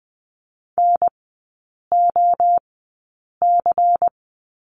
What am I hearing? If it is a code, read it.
Morse code, heard as a keyed tone, about 20 wpm, spaced as for 8 wpm, 700 Hz. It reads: NOC
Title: 138 Lesson 23e - Letter f - Three Characters 20wpm